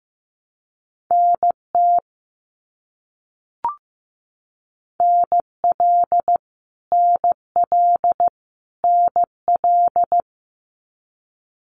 Morse code, heard as a keyed tone, about 15 wpm, 700 Hz. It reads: NT NL NL NL
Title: Canadian Provinces and Territories - Repeated 3x - 15wpm